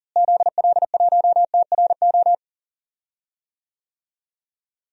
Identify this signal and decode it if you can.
Morse code, heard as a keyed tone, about 40 wpm, 700 Hz. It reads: ZL1TRO
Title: DXCC-100 Call Signs - 6 Characters - Repeated 2x - 40wpm